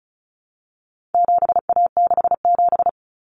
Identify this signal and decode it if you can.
Morse code, heard as a keyed tone, about 34 wpm, 700 Hz. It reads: 7A67
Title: Common Cut Numbers - 4-Digit 35wpm